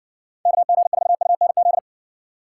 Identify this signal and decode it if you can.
Morse code, heard as a keyed tone, about 58 wpm, 700 Hz. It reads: KG4UNZ